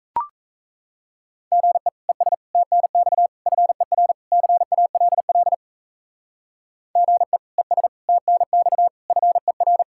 Morse code, heard as a keyed tone, about 35 wpm, 700 Hz. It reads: GE ES TNX FER CALL GE ES TNX FER
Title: Speed-Racing Common QSO Phrases - 28wpm